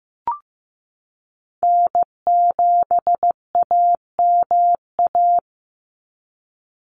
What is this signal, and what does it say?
Morse code, heard as a keyed tone, about 15 wpm, 700 Hz. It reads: N7AMA
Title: NA SOTA Chaser Callsigns 15wpm